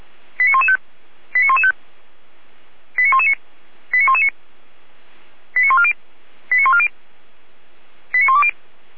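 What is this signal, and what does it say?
Five-tone selective call (ZVEI), heard as quick runs of five tones, tones of about 70 ms. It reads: 89115 89115 89119 89119 89140 89140 89120
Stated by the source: ZVEI five-tone selective call 89092 received on SDR.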